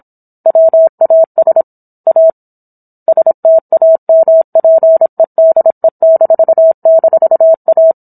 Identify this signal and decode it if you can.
Morse code, heard as a keyed tone, about 27 wpm, 650 Hz. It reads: WAS A STAMPEDE--A
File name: Book One / 17-Chapter 17 - 26 WPM